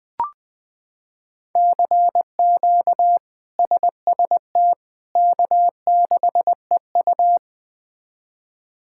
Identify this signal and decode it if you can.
Morse code, heard as a keyed tone, about 20 wpm, 700 Hz. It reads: CQ SST K6EU
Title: K1USN SST QSO Practice - 20wpm